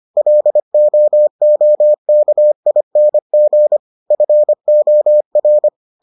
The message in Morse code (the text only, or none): LOOKING FOR